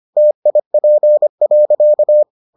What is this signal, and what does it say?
Morse code, heard as a keyed tone, about 25 wpm, 600 Hz. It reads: TIP.